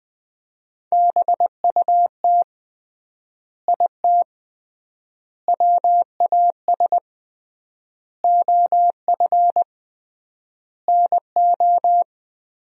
Morse code, heard as a keyed tone, about 21 wpm, 700 Hz. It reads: BUT IT WAS OF NO
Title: 3x Word-Spacing - Sentences from Top 100 Words - 20wpm